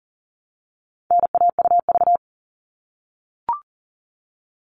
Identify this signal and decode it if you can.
Morse code, heard as a keyed tone, about 40 wpm, 700 Hz. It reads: NAUV
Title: All Cut Numbers - 4-Digit 40wpm